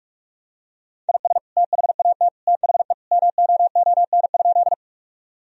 Morse code, heard as a keyed tone, about 46 wpm, 700 Hz. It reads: IS THAT THE MOON?
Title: Sentences from Top 1000 Words - Review Part 3 - 45wpm